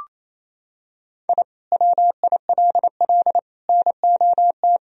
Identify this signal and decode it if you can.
Morse code, heard as a keyed tone, about 28 wpm, 700 Hz. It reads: I WILL NOT
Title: Sentences from Top 1000 Words - Review Part 1 - 28wpm